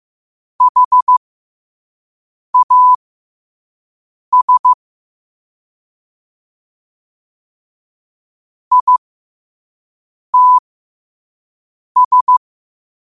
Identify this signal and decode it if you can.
Morse code, heard as a keyed tone, about 15 wpm, spaced as for 5 wpm, 1000 Hz. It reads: HAS ITS
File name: qotdfiles / 5wpm